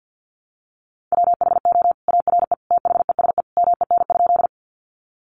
Morse code, heard as a keyed tone, about 48 wpm, 700 Hz. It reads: WHO ARE THESE MEN?
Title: Sentences from Top 200 Words 50wpm